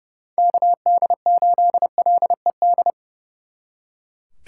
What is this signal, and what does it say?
Morse code, heard as a keyed tone, about 37 wpm, 700 Hz. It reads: KD8LED
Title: Speed Racing - USA 2x3 Call Signs 30wpm